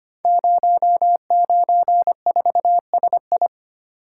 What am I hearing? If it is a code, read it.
Morse code, heard as a keyed tone, about 25 wpm, 700 Hz. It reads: 094SI
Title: Code Groups - 5 Letter:Numbers - 25wpm